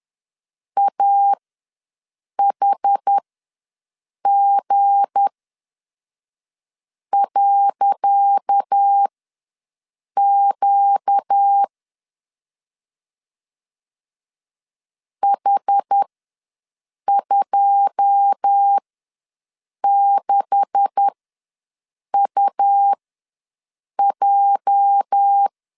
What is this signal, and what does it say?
Morse code, heard as a keyed tone, about 10 wpm, spaced as for 5 wpm, 800 Hz. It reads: AHG.Q H26UJ
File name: LEZIONI / 10 SIGNS / 10C DOT ws 10wpm 4213